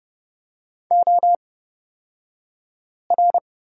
Morse code, heard as a keyed tone, about 29 wpm, spaced as for 5 wpm, 700 Hz. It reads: OR
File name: Lesson 37d - Letter x - Two Characters 30wpm